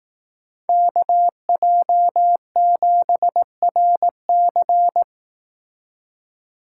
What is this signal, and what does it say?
Morse code, heard as a keyed tone, about 18 wpm, 700 Hz. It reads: KJ7RC
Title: Speed Racing - USA Call Signs 15wpm